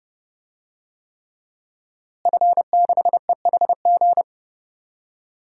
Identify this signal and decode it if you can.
Morse code, heard as a keyed tone, about 32 wpm, 700 Hz. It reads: F6EHG
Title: DXCC-100 Call Signs - 5 Characters - Repeated 3x - 30wpm